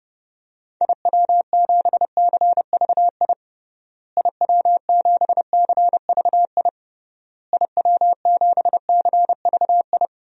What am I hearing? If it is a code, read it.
Morse code, heard as a keyed tone, about 31 wpm, 700 Hz. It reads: IW7CVI IW7CVI IW7CVI
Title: DXCC-100 Call Signs - 6 Characters - Repeated 3x - 30wpm